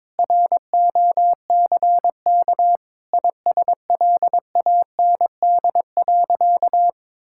Morse code, heard as a keyed tone, about 22 wpm, 700 Hz. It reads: ROCK ISLAND.
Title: Sentences from Top 1600 Words 22wpm